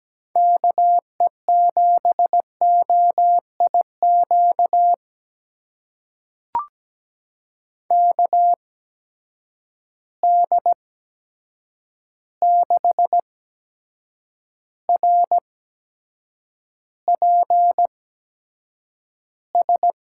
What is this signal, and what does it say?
Morse code, heard as a keyed tone, about 17 wpm, spaced as for 7 wpm, 700 Hz. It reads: KE7OIQ KD6RPS
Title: USA Call Signs - 6 Characters - 8x Character Spacing - 17wpm